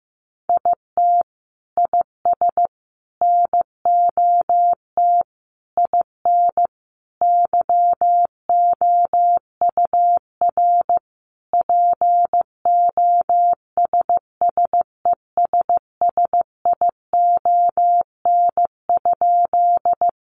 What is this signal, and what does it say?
Morse code, heard as a keyed tone, about 15 wpm, 700 Hz. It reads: IT IS NOT IN YOUR POSSESSION?
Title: Sentences from Top 1400 Words 15wpm